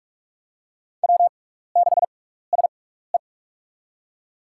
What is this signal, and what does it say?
Morse code, heard as a keyed tone, about 44 wpm, spaced as for 15 wpm, 700 Hz. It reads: W6SE
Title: USA Call Signs - 6x Character Spacing - 45wpm